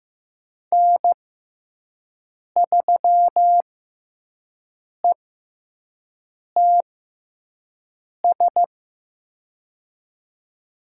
Morse code, heard as a keyed tone, about 15 wpm, spaced as for 5 wpm, 700 Hz. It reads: N3ETS
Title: USA Call Signs - 6x Character Spacing - 15wpm